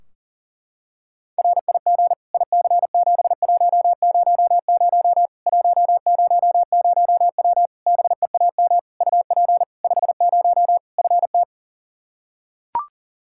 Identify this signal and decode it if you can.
Morse code, heard as a keyed tone, about 40 wpm, 700 Hz. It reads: RIG IC7100 100W BEAM UP 50 FT
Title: Practice QSO - Exchange 2 - Info - 40wpm